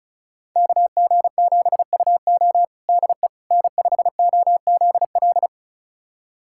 Morse code, heard as a keyed tone, about 36 wpm, 700 Hz. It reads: KG7UO DE N5OZL
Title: USA Callsign DE Callsign - 5 Characters - 35wpm